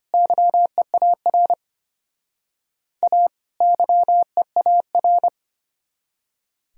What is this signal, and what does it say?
Morse code, heard as a keyed tone, about 32 wpm, 700 Hz. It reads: YEAR A YEAR